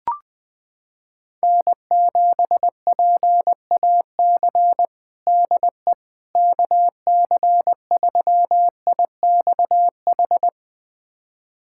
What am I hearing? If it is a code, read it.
Morse code, heard as a keyed tone, about 20 wpm, 700 Hz. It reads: N7PAC DE KC3IXH
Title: USA Callsign DE Callsign 20wpm